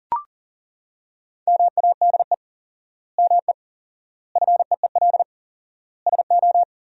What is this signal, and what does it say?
Morse code, heard as a keyed tone, about 40 wpm, 700 Hz. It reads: MADE ME FEEL SO
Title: 4x Word-Spacing - Sentences from Top 400 Words - 40wpm